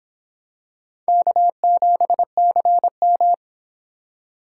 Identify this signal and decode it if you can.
Morse code, heard as a keyed tone, about 24 wpm, 700 Hz. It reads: K7CM